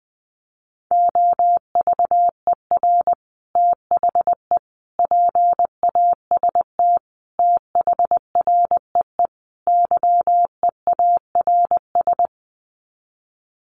Morse code, heard as a keyed tone, about 20 wpm, 700 Hz. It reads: OVER THE PAST THREE YEARS
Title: Sets of 5 Words Encore2 20wpm